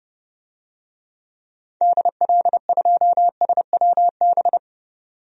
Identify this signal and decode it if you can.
Morse code, heard as a keyed tone, about 30 wpm, 700 Hz. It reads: DL2SWB
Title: DXCC-100 Call Signs - 6 Characters - 30wpm